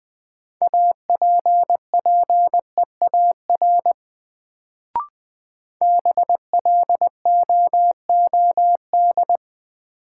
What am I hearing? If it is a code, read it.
Morse code, heard as a keyed tone, about 20 wpm, 700 Hz. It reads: APPEAR BLOOD
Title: Sentences from Top 1000 Words - Review Part 2 - New Word Drill - 20wpm